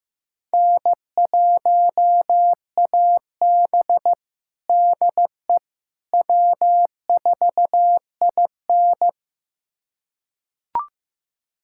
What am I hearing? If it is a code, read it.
Morse code, heard as a keyed tone, about 15 wpm, 700 Hz. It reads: N1AB DE W4IN
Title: USA Callsign DE Callsign - 4 Characters - 15wpm